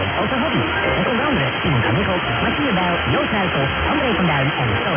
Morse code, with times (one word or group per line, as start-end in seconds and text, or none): none